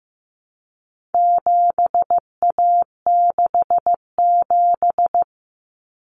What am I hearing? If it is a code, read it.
Morse code, heard as a keyed tone, about 15 wpm, 700 Hz. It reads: 7A67
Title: Common Cut Numbers - 4-Digit 15wpm